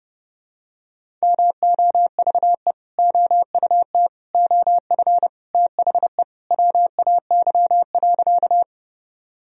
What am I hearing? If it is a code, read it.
Morse code, heard as a keyed tone, about 30 wpm, 700 Hz. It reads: MOVE OUT OF THE WAY.
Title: Sentences from Top 1000 Words - Review Part 1 - 30wpm